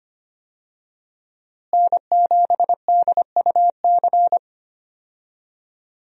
Morse code, heard as a keyed tone, about 25 wpm, 700 Hz. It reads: N7DUC
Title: USA Call Signs - 5 Characters - Repeated 3x - 25wpm